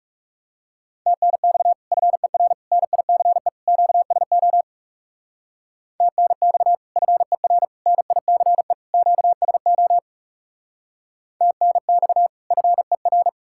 Morse code, heard as a keyed tone, about 37 wpm, 700 Hz. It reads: TNX FER NICE QSO TNX FER NICE QSO TNX FER
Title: Speed-Racing Common QSO Phrases - 30wpm